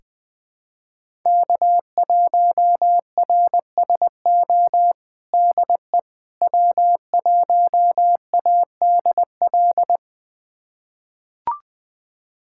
Morse code, heard as a keyed tone, about 20 wpm, 700 Hz. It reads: K1RSO DE W1ADL
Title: USA Callsign DE Callsign - 5 Characters - 20wpm